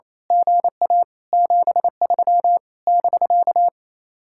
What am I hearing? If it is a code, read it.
Morse code, heard as a keyed tone, about 27 wpm, 700 Hz. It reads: GA 73 <BK>
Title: Speed-Racing Common QSO Phrases - 28wpm